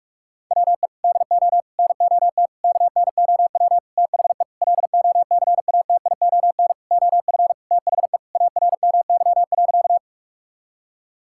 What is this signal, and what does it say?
Morse code, heard as a keyed tone, about 41 wpm, 690 Hz. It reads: WE DO NOT KNOW THE LOCATION OF THE ARMY.
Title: Sentences from Top 1000 Words - Review Part 6 - 45wpm